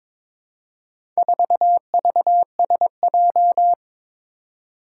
Morse code, heard as a keyed tone, about 22 wpm, 700 Hz. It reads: 4VSJ
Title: Four Challenging Letter-Number Combinations 22wpm